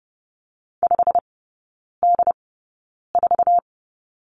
Morse code, heard as a keyed tone, about 30 wpm, spaced as for 9 wpm, 700 Hz. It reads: 5D4
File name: course / 30:10wpm / 087 Lesson 15e - Number 5 - Three Characters 30wpm